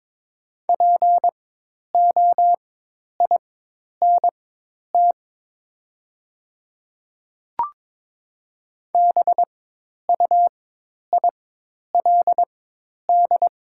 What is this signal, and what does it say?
Morse code, heard as a keyed tone, about 22 wpm, spaced as for 10 wpm, 700 Hz. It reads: POINT BUILD
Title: 4x Character Spacing - Top 200 Words - 22wpm